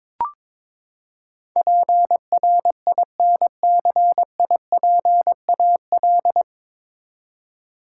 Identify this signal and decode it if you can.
Morse code, heard as a keyed tone, about 21 wpm, 700 Hz. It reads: PRINCIPAL
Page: Morse Code Ninja practice 22wpm